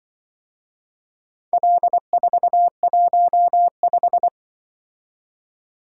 Morse code, heard as a keyed tone, about 24 wpm, 700 Hz. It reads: L415
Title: Speed-Racing - Four Letter-Number Combinations - 20wpm